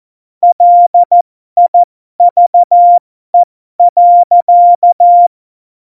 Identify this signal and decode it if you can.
Morse code, heard as a keyed tone, about 14 wpm, spaced as for 12 wpm, 700 Hz. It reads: LIVE.
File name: Morsekurs / Witze